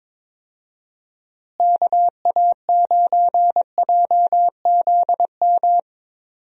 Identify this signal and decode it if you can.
Morse code, heard as a keyed tone, about 22 wpm, 700 Hz. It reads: KA9JZM
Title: USA Call Signs - 6 Characters - Repeated 2x - 22wpm